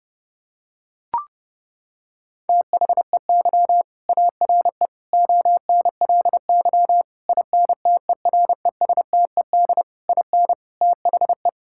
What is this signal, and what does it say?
Morse code, heard as a keyed tone, about 30 wpm, 700 Hz. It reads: THEY ARE ONLY INTERESTED IN THE
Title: Sentences from Top 1000 Words - Review Part 6 - 30wpm